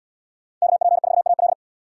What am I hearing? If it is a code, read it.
Morse code, heard as a keyed tone, about 66 wpm, 700 Hz. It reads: KP4AL